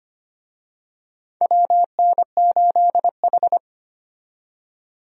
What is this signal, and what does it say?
Morse code, heard as a keyed tone, about 25 wpm, 700 Hz. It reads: WN8H
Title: USA Call Signs - 4 Characters - 25wpm